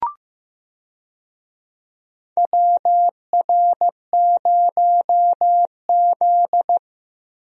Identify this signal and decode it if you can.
Morse code, heard as a keyed tone, about 15 wpm, 700 Hz. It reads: WR0Z
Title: USA Call Signs - 4 Characters - Repeated 2x - 15wpm